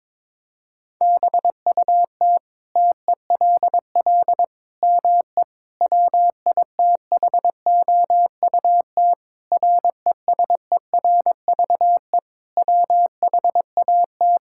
Morse code, heard as a keyed tone, about 22 wpm, 700 Hz. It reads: BUT TELL ME WITHOUT RESERVE WHAT
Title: Sentences from Top 1000 Words - Review Part 6 - 22wpm